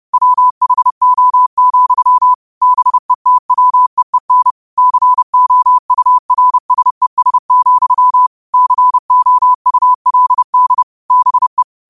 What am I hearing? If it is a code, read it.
Morse code, heard as a keyed tone, about 30 wpm, 1000 Hz. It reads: WHO, BETWEEN COURSES, COULD BE